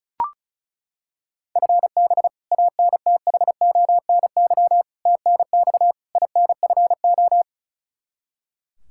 Morse code, heard as a keyed tone, about 34 wpm, 700 Hz. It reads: FB ANTHONY TNX INFO
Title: Practice QSO - Exchange 3 - More Info - 35wpm